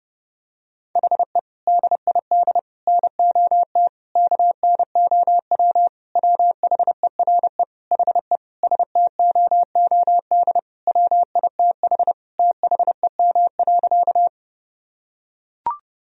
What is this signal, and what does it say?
Morse code, heard as a keyed tone, about 30 wpm, 700 Hz. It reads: HE DID NOT KNOW WHERE HE STOOD WITH THEM.
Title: Sentences from Top 1000 Words - Review Part 2 - 30wpm